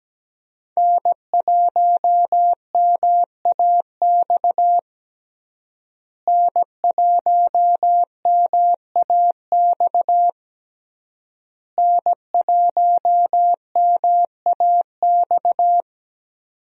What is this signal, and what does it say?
Morse code, heard as a keyed tone, about 17 wpm, 700 Hz. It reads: N1MAX N1MAX N1MAX
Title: USA Call Signs - 5 Characters - Repeated 3x - 17wpm